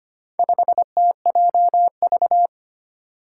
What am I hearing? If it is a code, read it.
Morse code, heard as a keyed tone, about 25 wpm, 700 Hz. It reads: 5TJV
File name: practice / 25wpm / Four Letter-Number Combinations - 25wpm